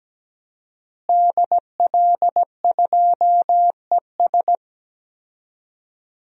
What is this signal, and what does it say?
Morse code, heard as a keyed tone, about 17 wpm, 700 Hz. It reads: DL2ES
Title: DXCC-100 Call Signs - 5 Characters - Repeated 2x - 17wpm